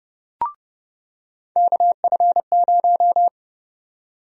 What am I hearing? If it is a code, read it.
Morse code, heard as a keyed tone, about 30 wpm, 700 Hz. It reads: KF0
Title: USA Call Sign Prefixes - 30wpm